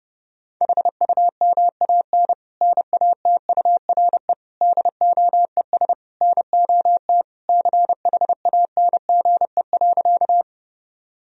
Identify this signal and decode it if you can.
Morse code, heard as a keyed tone, about 30 wpm, 700 Hz. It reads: HUMAN NATURE DOES NOT CHANGE.
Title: Sentences from Top 1000 Words - Review Part 3 - 30wpm